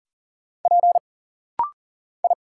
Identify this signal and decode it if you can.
Morse code, heard as a keyed tone, about 40 wpm, 700 Hz. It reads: P I
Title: Single Letter-Number Alt - Rapid-Fire 40wpm